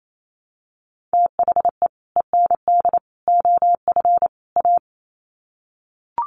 Morse code, heard as a keyed tone, about 28 wpm, 700 Hz. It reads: THE END OF A
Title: Sets of 4 Words 28wpm